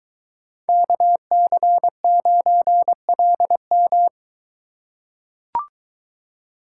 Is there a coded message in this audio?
Yes: Morse code from 0.7 to 4.1 s, heard as a keyed tone, about 22 wpm, 700 Hz. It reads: KC9LM